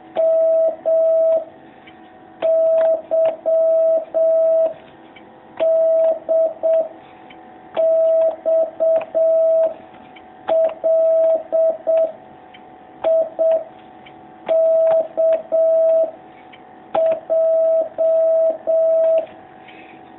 Morse code, heard as a keyed tone, about 7 wpm, 650 Hz. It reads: M Y D X L I K J
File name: jh8cbh / 2019cw